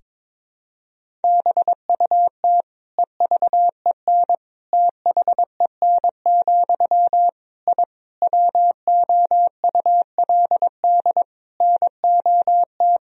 Morse code, heard as a keyed tone, about 22 wpm, 700 Hz. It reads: BUT EVEN THEN, I WOULD NOT
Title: Sentences from Top 1000 Words - Review Part 1 - 22wpm